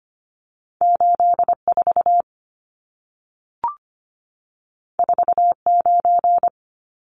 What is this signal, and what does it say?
Morse code, heard as a keyed tone, about 25 wpm, 700 Hz. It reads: 84 49
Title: Numbers - 2-Digit 25wpm